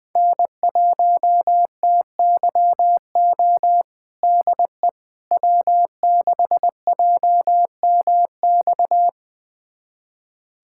Morse code, heard as a keyed tone, about 20 wpm, 700 Hz. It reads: N1TYO DE W6JMX
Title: USA Callsign DE Callsign - 5 Characters - 20wpm